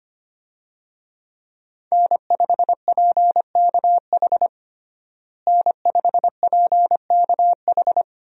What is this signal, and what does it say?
Morse code, heard as a keyed tone, about 25 wpm, 700 Hz. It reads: N5PKH N5PKH
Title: USA Call Signs - 5 Characters - Repeated 2x - 25wpm